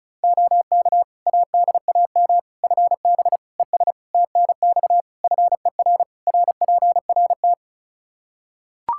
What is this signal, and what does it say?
Morse code, heard as a keyed tone, about 36 wpm, 700 Hz. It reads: OK ADAM FB ES TNX FER RPRT
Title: Practice QSO - Exchange 2 - Info - 35wpm